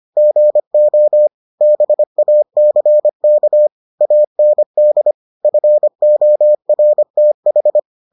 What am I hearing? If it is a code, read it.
Morse code, heard as a keyed tone, about 25 wpm, 600 Hz. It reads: GO BACK AND FORTH